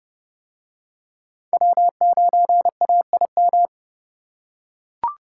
Morse code, heard as a keyed tone, about 27 wpm, 700 Hz. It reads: W9AIM